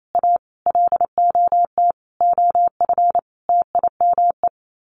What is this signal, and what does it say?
Morse code, heard as a keyed tone, about 28 wpm, 700 Hz. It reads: A LOT OF TIME
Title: Sets of 5 Words 28wpm